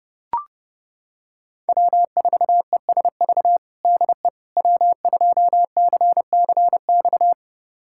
Morse code, heard as a keyed tone, about 30 wpm, 700 Hz. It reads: W4ESV DE W2CCX
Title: USA Callsign DE Callsign - 5 Characters - 30wpm